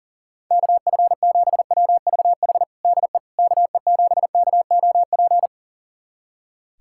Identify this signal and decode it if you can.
Morse code, heard as a keyed tone, about 40 wpm, 700 Hz. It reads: KF7WVH DE KE7KOP
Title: USA Callsign DE Callsign - 6 Characters - 40wpm